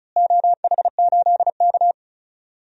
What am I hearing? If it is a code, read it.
Morse code, heard as a keyed tone, about 36 wpm, 690 Hz. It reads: OH8K
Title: DXCC-100 Call Signs - 4 Characters - Repeated 3x - 35wpm